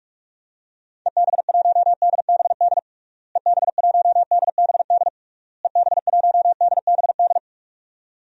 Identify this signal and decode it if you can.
Morse code, heard as a keyed tone, about 46 wpm, 700 Hz. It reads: EB1DBD EB1DBD EB1DBD
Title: DXCC-100 Call Signs - 6 Characters - Repeated 3x - 45wpm